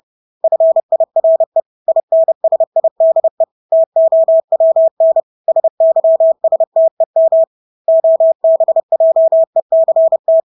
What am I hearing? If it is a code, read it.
Morse code, heard as a keyed tone, about 30 wpm, 650 Hz. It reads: FIRE INSIDE TOWN SYSTEM OBJECT
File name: Engelske ord 30-30 wpm